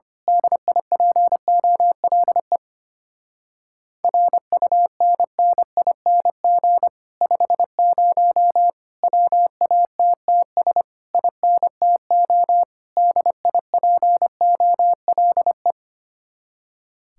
Morse code, heard as a keyed tone, about 24 wpm, 700 Hz. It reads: DIPOLE RUNNING 50 WATTS INTO DIPOLE
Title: Speed-Racing Common QSO Phrases - 25wpm